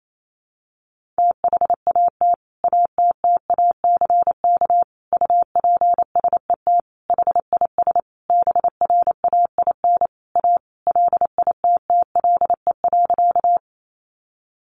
Morse code, heard as a keyed tone, about 28 wpm, 700 Hz. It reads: THAT ATTACK UPSET HIS BRAIN A LITTLE.
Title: Sentences from Top 1500 Words 28wpm